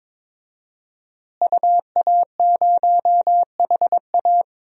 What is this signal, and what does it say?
Morse code, heard as a keyed tone, about 22 wpm, 700 Hz. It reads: UA0HA
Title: DXCC-100 Call Signs - 5 Characters - Repeated 3x - 22wpm